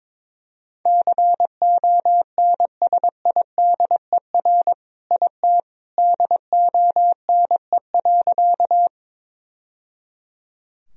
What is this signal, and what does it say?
Morse code, heard as a keyed tone, about 21 wpm, 700 Hz. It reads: CONSIDER IT DONE.
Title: Sentences from Top 1000 Words - Review Part 4 - 22wpm